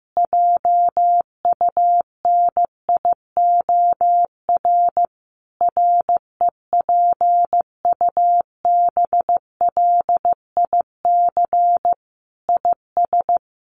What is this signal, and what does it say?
Morse code, heard as a keyed tone, about 15 wpm, 700 Hz. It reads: JUNIOR REPUBLIC IS
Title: Sentences from Top 1500 Words 15wpm